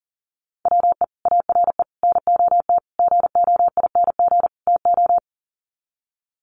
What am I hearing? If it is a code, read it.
Morse code, heard as a keyed tone, about 37 wpm, 700 Hz. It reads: WE ARE NOT GOING TO
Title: Sets of 5 Words 40wpm